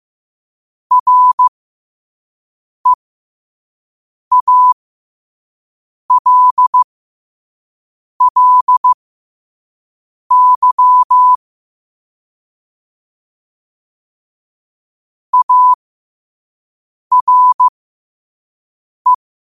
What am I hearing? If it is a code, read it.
Morse code, heard as a keyed tone, about 15 wpm, spaced as for 5 wpm, 1000 Hz. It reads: REALLY ARE
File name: qotdfiles / 5wpm